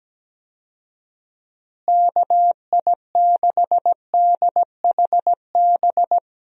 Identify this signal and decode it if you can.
Morse code, heard as a keyed tone, about 17 wpm, 700 Hz. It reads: KI6DHB
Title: USA Call Signs - 6 Characters - Repeated 3x - 17wpm